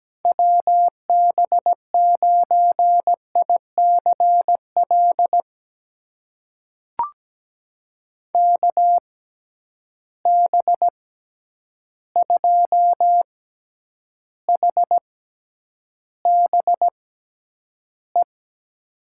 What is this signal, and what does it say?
Morse code, heard as a keyed tone, about 17 wpm, spaced as for 9 wpm, 700 Hz. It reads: WB9ICL KB2HBE
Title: USA Call Signs - 6 Characters - 6x Character Spacing - 17wpm